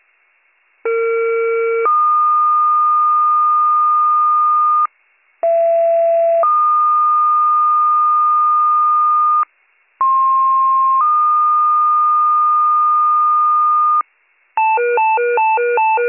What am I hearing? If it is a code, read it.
Two-tone page, heard as a one-second tone, then a three-second tone, about 460 then 1220 Hz, about 670 then 1220 Hz, about 1060 then 1220 Hz. It is heard from 0.8 to 14.0 s.